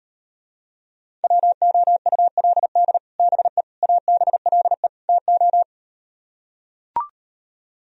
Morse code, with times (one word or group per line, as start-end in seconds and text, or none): 1.2-3.0 WOULD
3.2-3.6 BE
3.8-4.9 ABLE
5.1-5.6 TO